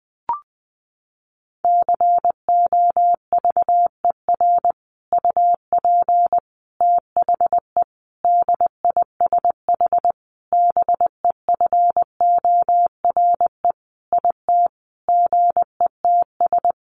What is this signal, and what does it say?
Morse code, heard as a keyed tone, about 20 wpm, 700 Hz. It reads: COVER UP THE DISH BEFORE IT GETS
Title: Sentences from Top 1300 Words 20wpm